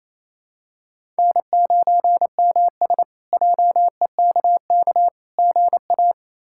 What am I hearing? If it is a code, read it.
Morse code, heard as a keyed tone, about 28 wpm, 700 Hz. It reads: N9MS JEKK GA
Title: K1USN SST QSO Practice - 28wpm